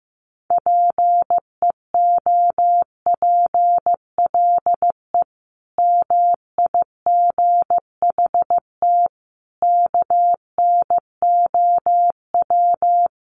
Morse code, heard as a keyed tone, about 15 wpm, 700 Hz. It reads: PEOPLE MIGHT KNOW
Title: Sentences from Top 500 Words 15wpm